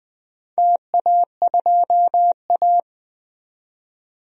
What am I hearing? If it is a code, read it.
Morse code, heard as a keyed tone, about 20 wpm, 700 Hz. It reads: TA2A
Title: DXCC-100 Call Signs - 4 Characters - 20wpm